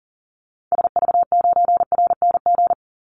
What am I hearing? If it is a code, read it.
Morse code, heard as a keyed tone, about 40 wpm, 700 Hz. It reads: SV9RNG
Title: Call Signs Repeated 3x - 40wpm